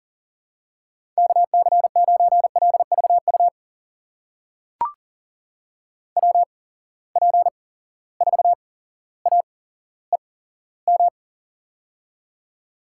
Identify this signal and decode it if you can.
Morse code, heard as a keyed tone, about 40 wpm, spaced as for 17 wpm, 700 Hz. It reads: KC9LVU WP4AEM
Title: USA Call Signs - 6 Characters - 8x Character Spacing - 40wpm